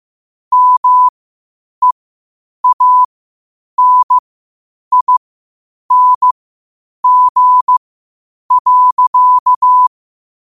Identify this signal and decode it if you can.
Morse code, heard as a keyed tone, about 15 wpm, spaced as for 8 wpm, 1000 Hz. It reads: MEANING.
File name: qotdfiles / 8wpm